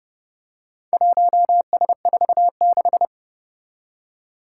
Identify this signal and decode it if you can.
Morse code, heard as a keyed tone, about 30 wpm, 700 Hz. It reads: 1S46
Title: Four Challenging Letter-Number Combinations 30wpm